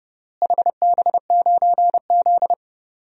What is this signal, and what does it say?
Morse code, heard as a keyed tone, about 31 wpm, 700 Hz. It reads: HB9Z
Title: DXCC-100 Call Signs - 4 Characters - Repeated 3x - 30wpm